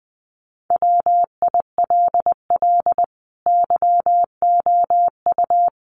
Morse code, heard as a keyed tone, about 19 wpm, 700 Hz. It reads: WILL YOU
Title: Sentences from Top 900 Words 20wpm